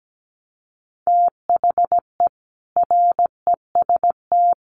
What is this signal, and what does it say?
Morse code, heard as a keyed tone, about 17 wpm, 700 Hz. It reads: THE REST
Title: Sets of 5 Words Encore 17wpm